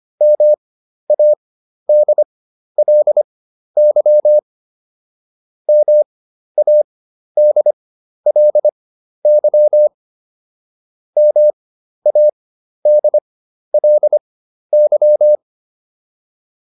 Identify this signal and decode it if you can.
Morse code, heard as a keyed tone, about 25 wpm, spaced as for 12 wpm, 600 Hz. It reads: MADLY MADLY MADLY